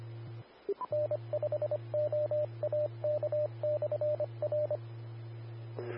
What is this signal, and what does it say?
Morse code, heard as a keyed tone, about 25 wpm, 600 Hz, with radio noise. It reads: N5OAK/R
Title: Austin, TX Amateur Repeaters